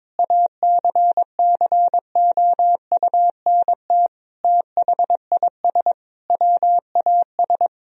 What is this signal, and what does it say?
Morse code, heard as a keyed tone, about 22 wpm, 700 Hz. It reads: ACCOUNT THIS WAS
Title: Sentences from Top 1000 Words - Review Part 1 - 22wpm